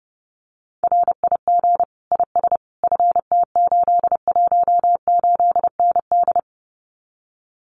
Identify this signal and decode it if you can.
Morse code, heard as a keyed tone, about 30 wpm, 700 Hz. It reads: RIG IS FT818ND
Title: Common QSO Phrases Encore 30wpm